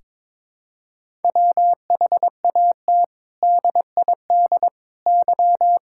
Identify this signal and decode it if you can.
Morse code, heard as a keyed tone, about 22 wpm, 700 Hz. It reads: WHAT DID Y
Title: Sentences from Top 1000 Words - Review Part 5 - 22wpm